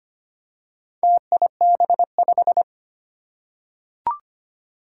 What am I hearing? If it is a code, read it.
Morse code, heard as a keyed tone, about 25 wpm, 700 Hz. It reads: TIB5
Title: Four Challenging Letter-Number Combinations 25wpm